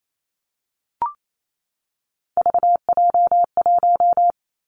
Morse code, heard as a keyed tone, about 28 wpm, 700 Hz. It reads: VJ1
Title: Three Challenging Letter-Number Combinations 28wpm